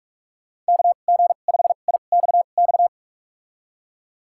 Morse code, heard as a keyed tone, about 46 wpm, spaced as for 33 wpm, 700 Hz. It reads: KG5IXX
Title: USA Call Signs - 2x Character Spacing - 45wpm